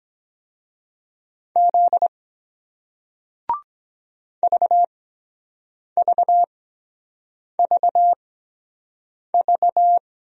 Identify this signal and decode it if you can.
Morse code, heard as a keyed tone, about 23 wpm, spaced as for 7 wpm, 700 Hz. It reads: Z VVVV